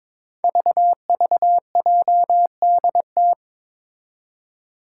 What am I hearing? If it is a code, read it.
Morse code, heard as a keyed tone, about 22 wpm, 700 Hz. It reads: VVJDT
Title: Five Challenging Letter-Number Combinations 22wpm